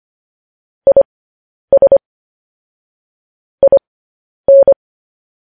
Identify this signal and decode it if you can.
Morse code, heard as a keyed tone, about 25 wpm, spaced as for 10 wpm, 550 Hz. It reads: IS IN
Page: news-headlines-in-morse-code-at-10-wpm